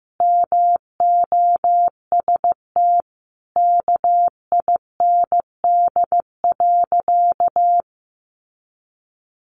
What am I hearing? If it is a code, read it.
Morse code, heard as a keyed tone, about 15 wpm, 700 Hz. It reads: MOST KIND.
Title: Sentences from Top 200 Words 15wpm